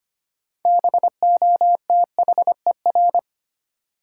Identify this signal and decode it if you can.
Morse code, heard as a keyed tone, about 25 wpm, 700 Hz. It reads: BOTHER
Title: Six Letter Words - 25wpm